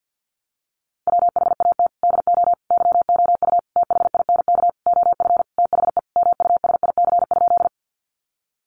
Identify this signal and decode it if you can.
Morse code, heard as a keyed tone, about 50 wpm, 700 Hz. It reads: WHAT DO YOU THINK OF THE MUSIC?
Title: Sentences from Top 300 Words 50wpm